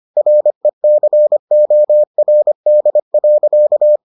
Morse code, heard as a keyed tone, about 25 wpm, 600 Hz. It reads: RECORD.